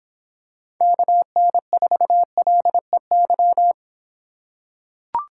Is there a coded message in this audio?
Yes: Morse code from 0.8 to 3.7 s, heard as a keyed tone, about 26 wpm, 700 Hz. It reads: KN4LEY